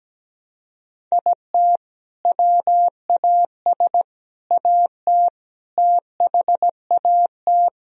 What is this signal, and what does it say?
Morse code, heard as a keyed tone, about 17 wpm, 700 Hz. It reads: IT WAS AT THAT
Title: Sentences from Top 1000 Words - Review Part 3 - 17wpm